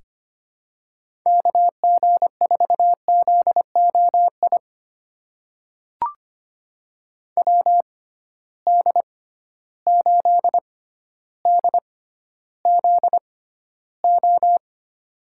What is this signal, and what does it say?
Morse code, heard as a keyed tone, about 25 wpm, spaced as for 13 wpm, 700 Hz. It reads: KG4ZOI WD8DZO
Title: USA Call Signs - 6 Characters - 6x Character Spacing - 25wpm